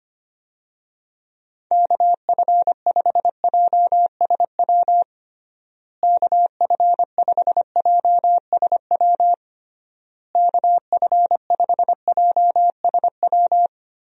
Morse code, heard as a keyed tone, about 25 wpm, 700 Hz. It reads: KF5JSW KF5JSW KF5JSW
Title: USA Call Signs - 6 Characters - Repeated 3x - 25wpm